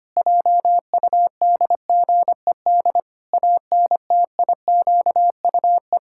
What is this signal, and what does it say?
Morse code, heard as a keyed tone, about 25 wpm, 700 Hz. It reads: JUDGED ANTIQUE
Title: Pangram Challenge v2 - 25wpm